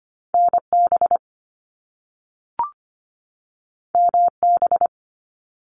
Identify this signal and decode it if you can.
Morse code, heard as a keyed tone, about 25 wpm, 700 Hz. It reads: NB MB
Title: Canadian Provinces and Territories 25wpm